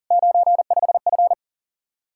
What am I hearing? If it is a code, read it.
Morse code, heard as a keyed tone, about 50 wpm, 700 Hz. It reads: 95F